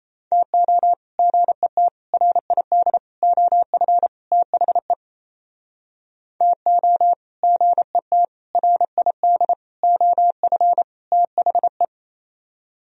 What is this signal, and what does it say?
Morse code, heard as a keyed tone, about 37 wpm, 700 Hz. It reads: TO GET RID OF THE TO GET RID OF THE